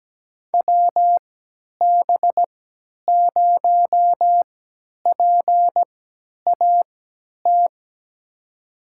Morse code, heard as a keyed tone, about 17 wpm, spaced as for 10 wpm, 700 Hz. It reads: WB0PAT
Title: USA Call Signs - 6 Characters - 3x Character Spacing - 17wpm